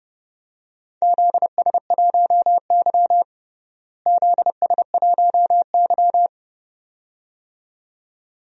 Morse code, heard as a keyed tone, about 30 wpm, 700 Hz. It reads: ZS1Y ZS1Y
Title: DXCC-100 Call Signs - 4 Characters - Repeated 2x - 30wpm